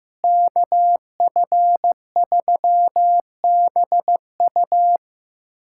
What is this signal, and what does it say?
Morse code, heard as a keyed tone, about 15 wpm, 700 Hz. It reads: KF3BU
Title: USA Callsign DE Callsign - 5 Characters - 15wpm